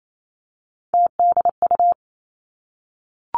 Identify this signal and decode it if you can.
Morse code, heard as a keyed tone, about 28 wpm, 700 Hz. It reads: TDU
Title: Three Challenging Letter-Number Combinations 28wpm